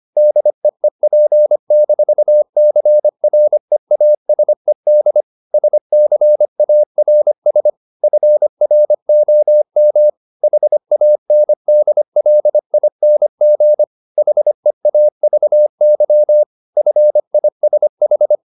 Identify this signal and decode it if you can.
Morse code, heard as a keyed tone, about 25 wpm, 600 Hz. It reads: DEEP-CREASED SCARS FROM HANDLING HEAVY FISH
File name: the old man and the sea part 1 25wpm